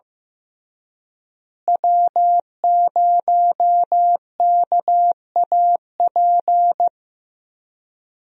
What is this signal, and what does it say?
Morse code, heard as a keyed tone, about 15 wpm, 700 Hz. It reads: W0KAP
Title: USA Call Signs - 5 Characters - Repeated 3x - 15wpm